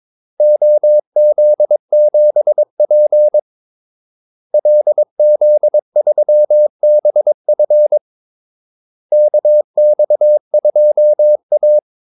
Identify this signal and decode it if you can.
Morse code, heard as a keyed tone, about 22 wpm, 600 Hz. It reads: OZ7P LZ3BF KX2A
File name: Callsigns-22wpm